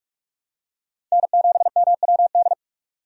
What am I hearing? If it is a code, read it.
Morse code, heard as a keyed tone, about 46 wpm, 700 Hz. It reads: N7MWD
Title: USA Call Signs - 5 Characters - 45wpm